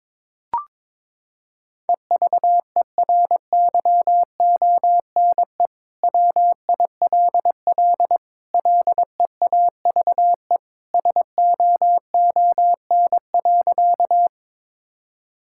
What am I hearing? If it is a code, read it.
Morse code, heard as a keyed tone, about 22 wpm, 700 Hz. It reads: EVERYONE WILL LEAVE SOON.
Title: Sentences from Top 1000 Words - Review Part 5 - 22wpm